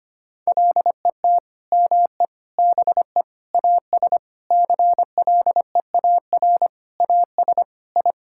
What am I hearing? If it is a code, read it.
Morse code, heard as a keyed tone, about 25 wpm, 700 Hz. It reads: LET ME BE AS CLEAR AS I
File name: practice / 25wpm / Sentences with 9-Word Phrases - 25wpm